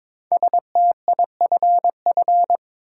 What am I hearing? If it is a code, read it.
Morse code, heard as a keyed tone, about 22 wpm, 690 Hz. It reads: STIFF
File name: 5 Letter Words - First 3 Letters Given - 22wpm